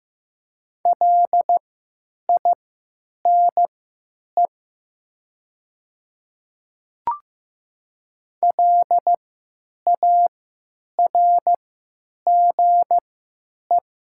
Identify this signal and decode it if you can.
Morse code, heard as a keyed tone, about 15 wpm, spaced as for 9 wpm, 700 Hz. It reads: LINE LARGE
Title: 3x Character Spacing - Top 300 Words - 15wpm